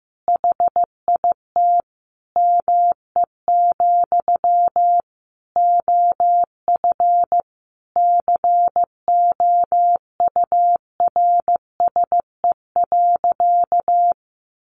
Morse code, heard as a keyed tone, about 15 wpm, 700 Hz. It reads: HIT ME, OF COURSE.
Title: Sentences from Top 1200 Words 15wpm